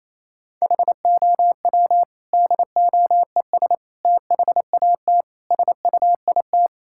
Morse code, heard as a keyed tone, about 28 wpm, 700 Hz. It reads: HOW DOES THAT SUIT
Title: Sentences from Top 1600 Words 28wpm